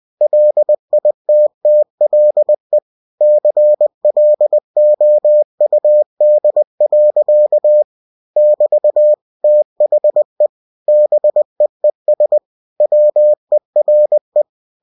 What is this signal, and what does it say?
Morse code, heard as a keyed tone, about 20 wpm, 600 Hz. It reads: LITTLE CLOUD. <BT> THE BEES WERE
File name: cw-books / winnie-the-pooh / winnie-20wpm-page-0009